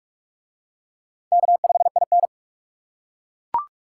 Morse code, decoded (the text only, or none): K5IN